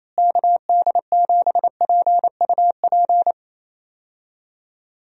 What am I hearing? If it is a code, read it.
Morse code, heard as a keyed tone, about 28 wpm, 700 Hz. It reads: KD7PUP
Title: USA Call Signs - 6 Characters - Repeated 2x - 28wpm